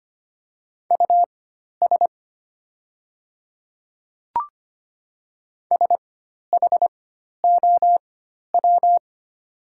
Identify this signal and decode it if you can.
Morse code, heard as a keyed tone, about 25 wpm, spaced as for 12 wpm, 690 Hz. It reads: US SHOW
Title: 4x Character Spacing - Top 200 Words - 25wpm